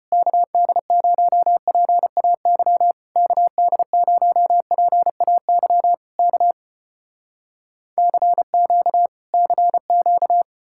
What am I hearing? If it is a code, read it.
Morse code, heard as a keyed tone, about 33 wpm, 700 Hz. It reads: KD0PAY KD0PAY K CQ CQ